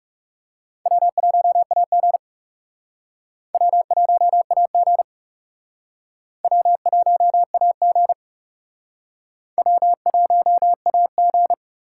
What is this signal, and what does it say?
Morse code, heard as a keyed tone, about 35 wpm, 700 Hz. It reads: W1AG W1AG W1AG W1AG
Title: Speed Racing - USA 1x2 Call Signs 30wpm